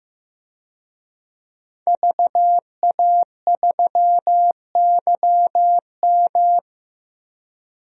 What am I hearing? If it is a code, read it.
Morse code, heard as a keyed tone, about 15 wpm, 700 Hz. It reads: VA3YM
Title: DXCC-100 Call Signs - 5 Characters - 15wpm